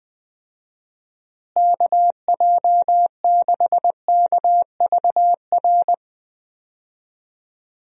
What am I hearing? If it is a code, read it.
Morse code, heard as a keyed tone, about 20 wpm, 700 Hz. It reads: KJ6KVR
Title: USA Call Signs - 6 Characters - Repeated 2x - 20wpm